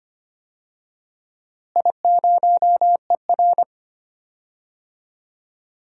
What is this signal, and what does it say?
Morse code, heard as a keyed tone, about 25 wpm, 700 Hz. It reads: I0ER
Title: DXCC-100 Call Signs - 4 Characters - Repeated 2x - 25wpm